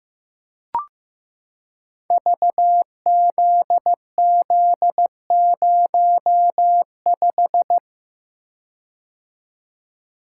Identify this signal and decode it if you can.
Morse code, heard as a keyed tone, about 15 wpm, 700 Hz. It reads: VZZ05
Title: Code Groups - 5 Letter:Numbers - 15wpm